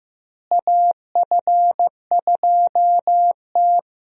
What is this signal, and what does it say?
Morse code, heard as a keyed tone, about 15 wpm, 700 Hz. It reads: AF2T
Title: USA Call Signs - 4 Characters - Repeated 3x - 15wpm